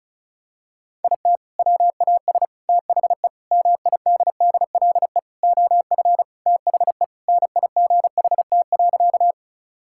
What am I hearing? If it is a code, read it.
Morse code, heard as a keyed tone, about 36 wpm, 700 Hz. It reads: IT WAS THE MIDDLE OF THE NIGHT.
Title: Sentences from Top 1000 Words - Review Part 6 - 35wpm